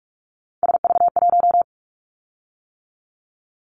Morse code, heard as a keyed tone, about 46 wpm, 700 Hz. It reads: SV1
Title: Callsign Prefixes for Top 50 DXCC - 45wpm